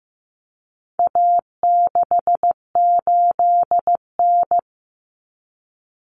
Morse code, heard as a keyed tone, about 15 wpm, 700 Hz. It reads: A68N
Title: Common Cut Numbers - 4-Digit 15wpm